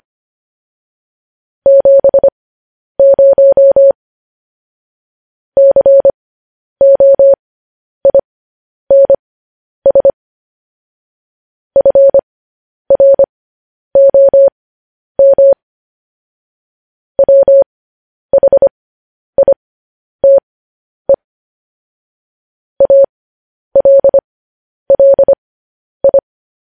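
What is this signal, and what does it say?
Morse code, heard as a keyed tone, about 25 wpm, spaced as for 10 wpm, 550 Hz. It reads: 70 COINS FROM WHITE ALLI
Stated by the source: Morse code practice sending news headlines at 10 wpm.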